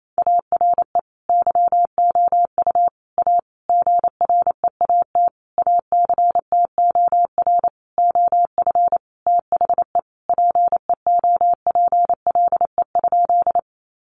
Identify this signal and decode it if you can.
Morse code, heard as a keyed tone, about 28 wpm, 700 Hz. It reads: ARE YOU A GREAT ACTOR OF THE PEOPLE?
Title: Sentences from Top 1500 Words 28wpm